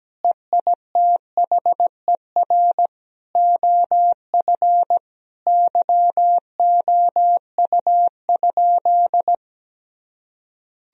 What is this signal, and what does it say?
Morse code, heard as a keyed tone, about 17 wpm, 700 Hz. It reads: EITHER OF YOU?
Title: Sentences from Top 1000 Words - Review Part 4 - 17wpm